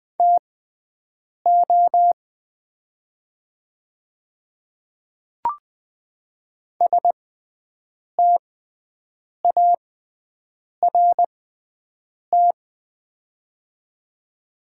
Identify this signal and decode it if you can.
Morse code, heard as a keyed tone, about 20 wpm, spaced as for 7 wpm, 700 Hz. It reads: TO START
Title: 6x Character Spacing - Top 300 Words - 20wpm